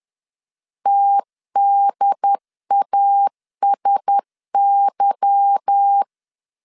Morse code, heard as a keyed tone, about 10 wpm, 800 Hz. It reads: TDASY
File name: CW / DET / 1 9 1500C S B 10wpm 4439